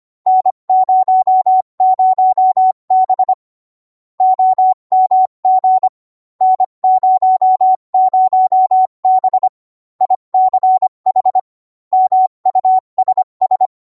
Morse code, heard as a keyed tone, about 25 wpm, 750 Hz. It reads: N00B OMG N00B ICH MUSS